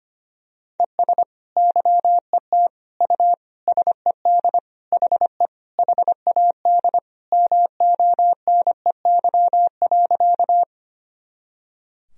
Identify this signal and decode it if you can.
Morse code, heard as a keyed tone, about 25 wpm, 700 Hz. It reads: ES YET U SED HE HAD MONEY.
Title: Sentences from Top 500 Words and Abbreviations 25wpm